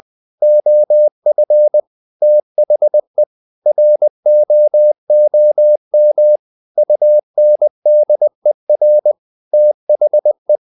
Morse code, heard as a keyed tone, about 20 wpm, 600 Hz. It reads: OF THE ROOM UNDER THE